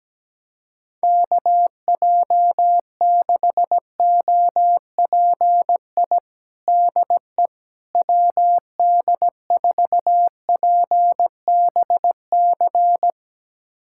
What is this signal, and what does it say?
Morse code, heard as a keyed tone, about 17 wpm, 700 Hz. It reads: KJ6OPI DE WD4PBC
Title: USA Callsign DE Callsign - 6 Characters - 17wpm